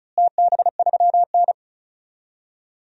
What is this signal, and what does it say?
Morse code, heard as a keyed tone, about 36 wpm, 700 Hz. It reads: TB3N
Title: DXCC-100 Call Signs - 4 Characters - Repeated 2x - 35wpm